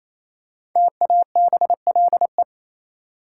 Morse code, heard as a keyed tone, about 27 wpm, 700 Hz. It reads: TABLE
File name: Easy Words - Speed-Racing with Hints - 22wpm